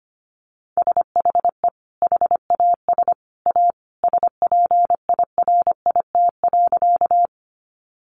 Morse code, heard as a keyed tone, about 25 wpm, 700 Hz. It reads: SHE HAS A SPIRIT.
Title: Sentences from Top 1400 Words 25wpm